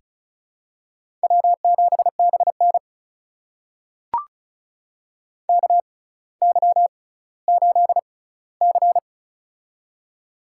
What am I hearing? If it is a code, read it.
Morse code, heard as a keyed tone, about 36 wpm, spaced as for 18 wpm, 700 Hz. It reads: W7BN KY8C
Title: USA Call Signs - 4 Characters - 6x Character Spacing - 35wpm